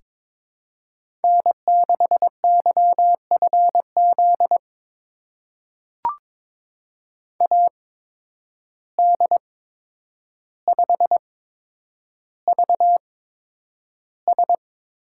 Morse code, heard as a keyed tone, about 22 wpm, spaced as for 9 wpm, 700 Hz. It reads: N6YFZ AD5VS
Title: USA Call Signs - 8x Character Spacing - 22wpm